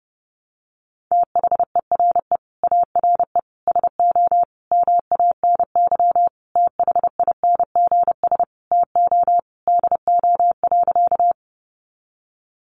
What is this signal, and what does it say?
Morse code, heard as a keyed tone, about 30 wpm, 700 Hz. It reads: THERE ARE SO MANY THINGS TO DO.
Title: Additional Easy Phrases 30wpm